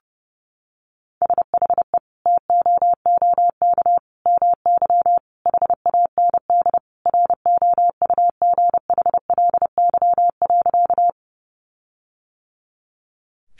Morse code, heard as a keyed tone, about 30 wpm, 700 Hz. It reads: SHE TOOK MY HAND ROUGHLY.
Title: Sentences from Top 1500 Words 30wpm